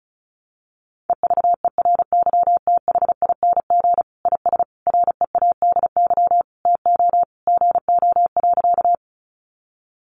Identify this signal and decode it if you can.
Morse code, heard as a keyed tone, about 34 wpm, 700 Hz. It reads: EVERYTHING IS READY TO GO.
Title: Sentences from Top 700 Words 35wpm